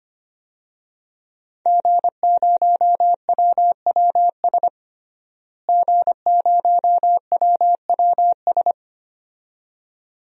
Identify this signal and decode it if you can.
Morse code, heard as a keyed tone, about 25 wpm, 700 Hz. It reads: G0WWS G0WWS
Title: DXCC-100 Call Signs - 5 Characters - Repeated 2x - 25wpm